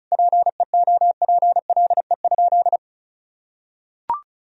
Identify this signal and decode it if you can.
Morse code, heard as a keyed tone, about 36 wpm, 700 Hz. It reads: PEOPLE?
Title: Sentences from Top 1000 Words - Review Part 1 - 35wpm